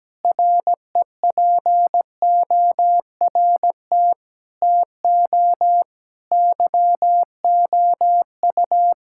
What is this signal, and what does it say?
Morse code, heard as a keyed tone, about 17 wpm, 700 Hz. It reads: REPORT TO YOU
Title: Sentences from Top 1000 Words - Review Part 6 - 17wpm